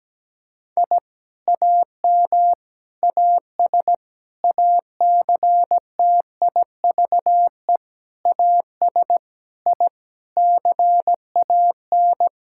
Morse code, heard as a keyed tone, about 17 wpm, 700 Hz. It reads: I AM AS ACTIVE AS I CAN
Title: Sentences from Top 1000 Words - Review Part 6 - 17wpm